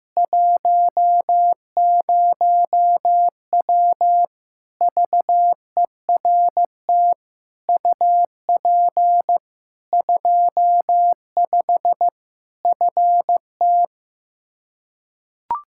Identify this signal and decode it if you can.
Morse code, heard as a keyed tone, about 15 wpm, 700 Hz. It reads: 10W VERT UP 25 FT
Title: Practice QSO - Exchange 2 - Info - 15wpm